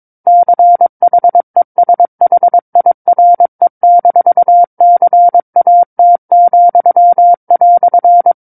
Morse code, heard as a keyed tone, about 22 wpm, 700 Hz. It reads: CHESHIRE-CAT,"
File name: 06-Chapter-06 22 WPM